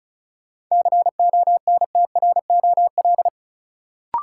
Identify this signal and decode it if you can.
Morse code, heard as a keyed tone, about 36 wpm, 700 Hz. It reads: CONTROL
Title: Sentences from Top 1000 Words - Review Part 2 - New Word Drill - 35wpm